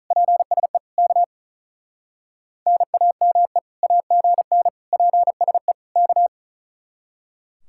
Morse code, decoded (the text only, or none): PSE K NAME AGN PSE K